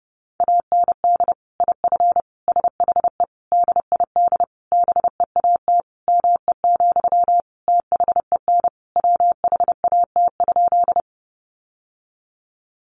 Morse code, heard as a keyed tone, about 30 wpm, 700 Hz. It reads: AND IF SHE DID BEAT ME, THEN WHAT?
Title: Sentences from Top 1300 Words 30wpm